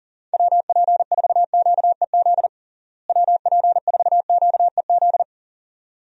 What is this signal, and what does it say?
Morse code, heard as a keyed tone, about 40 wpm, 700 Hz. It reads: WP4QEZ WP4QEZ
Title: USA Call Signs - 6 Characters - Repeated 2x - 40wpm